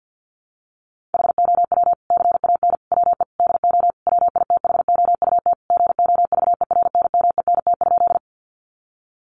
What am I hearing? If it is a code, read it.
Morse code, heard as a keyed tone, about 50 wpm, 700 Hz. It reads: HOW CAN WE DO WITHOUT GOVERNMENT?
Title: Sentences from Top 600 Words 50wpm